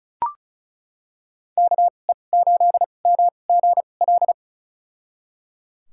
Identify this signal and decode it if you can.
Morse code, heard as a keyed tone, about 34 wpm, spaced as for 25 wpm, 700 Hz. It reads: KE8MGL
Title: USA Call Signs - 6 Characters - 2x Character Spacing - 35wpm